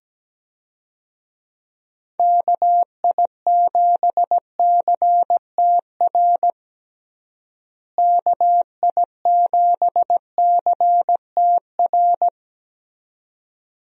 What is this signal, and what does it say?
Morse code, heard as a keyed tone, about 17 wpm, 700 Hz. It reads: KI7CTR KI7CTR
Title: USA Call Signs - 6 Characters - Repeated 2x - 17wpm